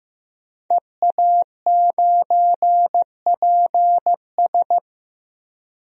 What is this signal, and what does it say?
Morse code, heard as a keyed tone, about 15 wpm, 700 Hz. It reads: EA9PS
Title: DXCC-100 Call Signs - 5 Characters - 15wpm